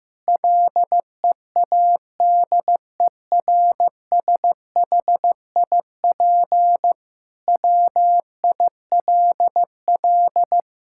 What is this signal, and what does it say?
Morse code, heard as a keyed tone, about 15 wpm, 700 Hz. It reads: LEADERSHIP WILL
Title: Sentences from Top 1600 Words 15wpm